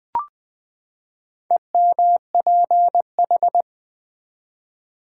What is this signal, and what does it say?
Morse code, heard as a keyed tone, about 20 wpm, 700 Hz. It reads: EMPH
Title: Build A Word - 4 Letters At A Time - 8-Letter Words - 20wpm